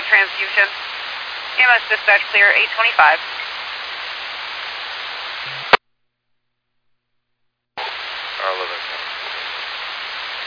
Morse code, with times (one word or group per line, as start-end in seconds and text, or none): none